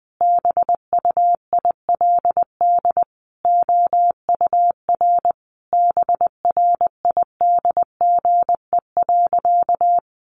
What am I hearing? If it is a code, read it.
Morse code, heard as a keyed tone, about 20 wpm, 700 Hz. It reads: BUILD OUR BRIDGE.
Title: Sentences from Top 1200 Words 20wpm